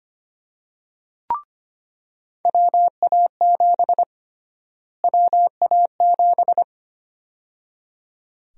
Morse code, heard as a keyed tone, about 25 wpm, 700 Hz. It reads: WA7 WA7
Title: USA Call Sign Prefixes - Repeated 2x - 25wpm